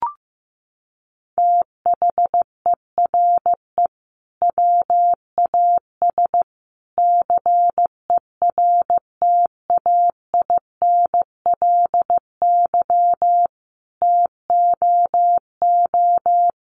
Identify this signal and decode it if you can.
Morse code, heard as a keyed tone, about 15 wpm, 700 Hz. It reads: THERE WAS CERTAINLY TOO
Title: Sentences from Top 900 Words 15wpm